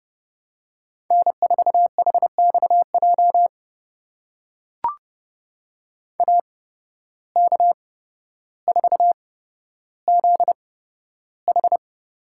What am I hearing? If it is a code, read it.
Morse code, heard as a keyed tone, about 30 wpm, spaced as for 13 wpm, 700 Hz. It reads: N4HXJ AK4ZH
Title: USA Call Signs - 5 Characters - 8x Character Spacing - 30wpm